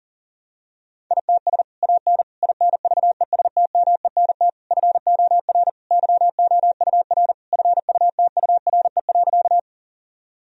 Morse code, heard as a keyed tone, about 40 wpm, 700 Hz. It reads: ITS AN INVESTMENT FOR YOUR FUTURE.
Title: Sentences from Top 1000 Words - Review Part 4 - 40wpm